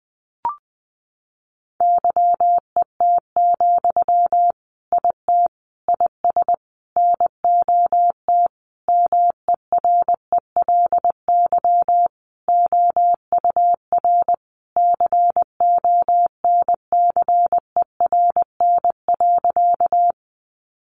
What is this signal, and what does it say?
Morse code, heard as a keyed tone, about 19 wpm, 700 Hz. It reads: YET, IT IS NOT MERELY OUR CONCERN.
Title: Sentences from Top 1400 Words 20wpm